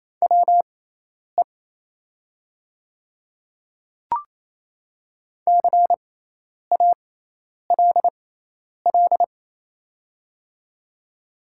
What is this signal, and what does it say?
Morse code, heard as a keyed tone, about 28 wpm, spaced as for 10 wpm, 700 Hz. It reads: WE CALL
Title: 6x Character Spacing - Top 200 Words - 28wpm